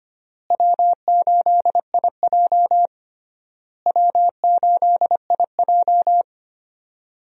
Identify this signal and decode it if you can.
Morse code, heard as a keyed tone, about 25 wpm, 700 Hz. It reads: W8IJ W8IJ
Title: USA Call Signs - 4 Characters - Repeated 3x - 25wpm